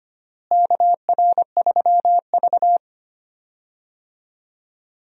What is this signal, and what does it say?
Morse code, heard as a keyed tone, about 25 wpm, 700 Hz. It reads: KR3V
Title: USA Call Signs - 4 Characters - Repeated 2x - 25wpm